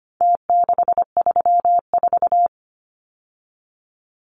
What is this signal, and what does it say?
Morse code, heard as a keyed tone, about 25 wpm, 700 Hz. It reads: T634
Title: Numbers - Contest Style - 4-Digit 25wpm